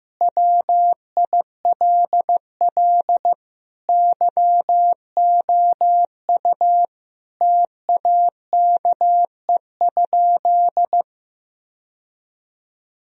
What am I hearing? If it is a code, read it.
Morse code, heard as a keyed tone, about 15 wpm, 700 Hz. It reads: WILL YOU TAKE?
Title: Sentences from Top 1000 Words - Review Part 2 - 15wpm